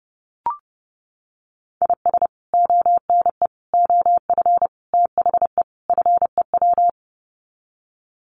Morse code, heard as a keyed tone, about 30 wpm, 700 Hz. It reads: IS ONE OF THE FEW
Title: Sets of 5 Words 30wpm